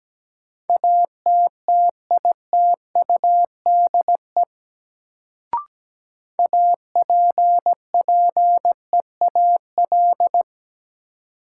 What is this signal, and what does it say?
Morse code, heard as a keyed tone, about 17 wpm, 700 Hz. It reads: ATTITUDE APPEAL
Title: Sentences from Top 1300 Words - New Word Drill Alt - 17wpm